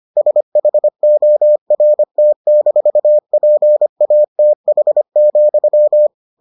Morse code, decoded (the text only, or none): SHORT-PATH,